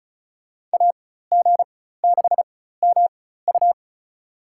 Morse code, heard as a keyed tone, about 36 wpm, spaced as for 16 wpm, 700 Hz. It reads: AG6MU
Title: USA Call Signs - 5 Characters - 4x Character Spacing - 35wpm